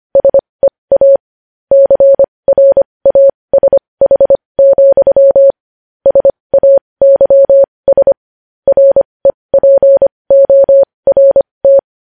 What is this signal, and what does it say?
Morse code, heard as a keyed tone, about 25 wpm, spaced as for 20 wpm, 550 Hz. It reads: SEA CRASH, SAYS REPORT
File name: podcast / News headlines-1x-25x20